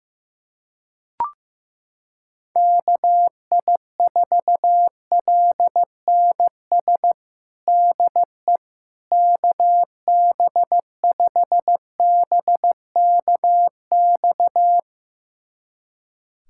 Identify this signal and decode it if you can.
Morse code, heard as a keyed tone, about 15 wpm, 700 Hz. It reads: KI4LNS DE KB5BKX
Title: USA Callsign DE Callsign 15wpm